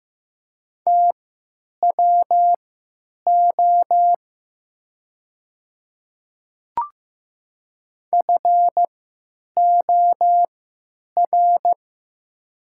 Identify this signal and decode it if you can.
Morse code, heard as a keyed tone, about 15 wpm, spaced as for 9 wpm, 700 Hz. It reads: TWO FOR
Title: 3x Character Spacing - Top 100 Words - 15wpm